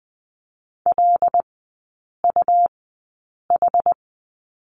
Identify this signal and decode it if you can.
Morse code, heard as a keyed tone, about 20 wpm, spaced as for 8 wpm, 700 Hz. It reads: LUH
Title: 178 Lesson 29e - Slash - Three Characters 20wpm